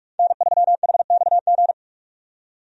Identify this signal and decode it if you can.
Morse code, heard as a keyed tone, about 43 wpm, 700 Hz. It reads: N3HXG
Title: USA Call Signs - 5 Characters - 45wpm